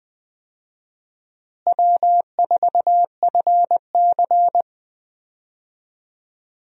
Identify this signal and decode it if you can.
Morse code, heard as a keyed tone, about 20 wpm, 700 Hz. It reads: W4FC
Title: USA Call Signs - 4 Characters - Repeated 2x - 20wpm